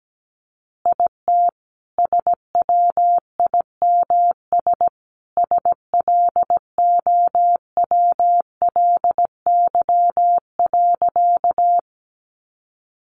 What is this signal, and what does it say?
Morse code, heard as a keyed tone, about 17 wpm, 700 Hz. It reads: IT SWIMS SLOWLY.
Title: Additional Easy Phrases 17wpm